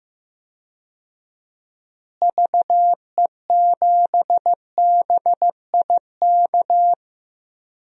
Morse code, heard as a keyed tone, about 15 wpm, 700 Hz. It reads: VE7BIK
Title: DXCC-100 Call Signs - 6 Characters - Repeated 3x - 15wpm